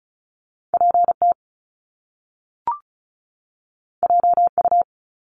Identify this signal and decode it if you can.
Morse code, heard as a keyed tone, about 36 wpm, 700 Hz. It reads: PT JU
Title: Top Two Letter Combinations in Words 35wpm